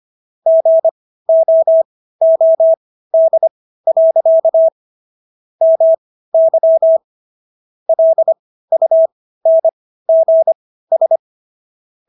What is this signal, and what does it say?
Morse code, heard as a keyed tone, about 25 wpm, spaced as for 15 wpm, 650 Hz. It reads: GOOD. MY LUNGS